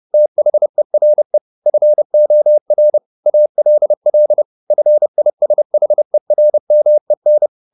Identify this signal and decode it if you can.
Morse code, heard as a keyed tone, about 30 wpm, 600 Hz. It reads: THERE FOR ALL FISHERMEN
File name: the old man and the sea / the old man and the sea 1 30wpm